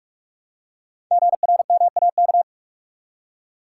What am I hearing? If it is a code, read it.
Morse code, heard as a keyed tone, about 44 wpm, 690 Hz. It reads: GRMAK
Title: Code Groups - 5 Letters - 45wpm